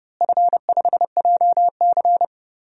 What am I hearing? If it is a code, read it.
Morse code, heard as a keyed tone, about 32 wpm, 700 Hz. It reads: F5JC
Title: DXCC-100 Call Signs - 4 Characters - Repeated 3x - 30wpm